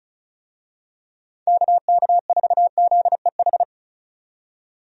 Morse code, heard as a keyed tone, about 34 wpm, 700 Hz. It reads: KK4ZEH
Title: USA Call Signs - 6 Characters - 35wpm